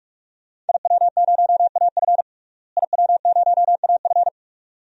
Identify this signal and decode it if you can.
Morse code, heard as a keyed tone, about 46 wpm, 700 Hz. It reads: IW0AF IW0AF
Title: DXCC-100 Call Signs - 5 Characters - Repeated 2x - 45wpm